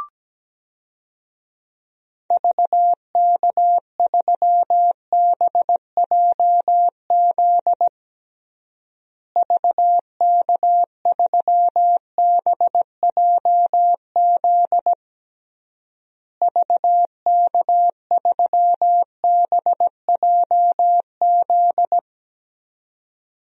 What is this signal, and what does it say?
Morse code, heard as a keyed tone, about 17 wpm, 700 Hz. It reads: VK3BJZ VK3BJZ VK3BJZ
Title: DXCC-100 Call Signs - 6 Characters - Repeated 3x - 17wpm